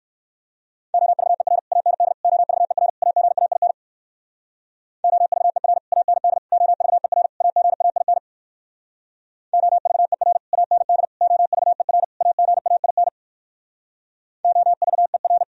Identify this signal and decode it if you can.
Morse code, heard as a keyed tone, about 58 wpm, 700 Hz. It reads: OVER AND OVER AGAIN OVER AND OVER AGAIN OVER AND OVER AGAIN OVER